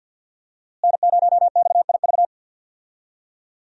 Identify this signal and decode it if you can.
Morse code, heard as a keyed tone, about 52 wpm, 700 Hz. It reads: N0XIV
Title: USA Call Signs - 5 Characters - Repeated 3x - 50wpm